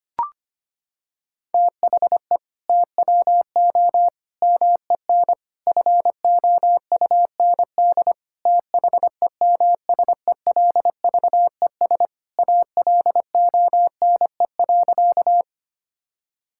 Morse code, heard as a keyed tone, about 25 wpm, 700 Hz. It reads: THE TWO MEN FOUND THEMSELVES ALONE.
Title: Sentences from Top 1000 Words - Review Part 5 - 25wpm